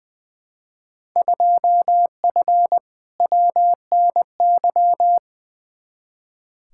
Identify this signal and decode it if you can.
Morse code, heard as a keyed tone, about 20 wpm, 700 Hz. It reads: 2F WNY
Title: Field Day Exchanges Encore - 20wpm